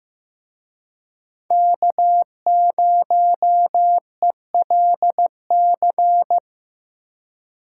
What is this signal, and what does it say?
Morse code, heard as a keyed tone, about 18 wpm, 700 Hz. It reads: K0ELC
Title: Speed Racing - USA Call Signs 15wpm